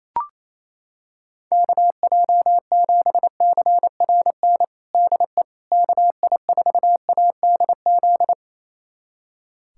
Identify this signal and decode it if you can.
Morse code, heard as a keyed tone, about 28 wpm, 700 Hz. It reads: KJ7CRN DE KI4ADZ
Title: USA Callsign DE Callsign - 6 Characters - 28wpm